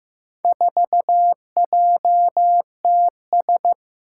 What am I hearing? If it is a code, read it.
Morse code, heard as a keyed tone, about 15 wpm, 700 Hz. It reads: 4JTS
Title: Four Challenging Letter-Number Combinations 15wpm